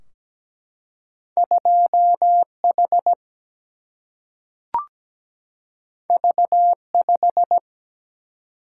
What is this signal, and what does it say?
Morse code, heard as a keyed tone, about 17 wpm, 700 Hz. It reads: 2H V5
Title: Single-Letter Single-Number 17wpm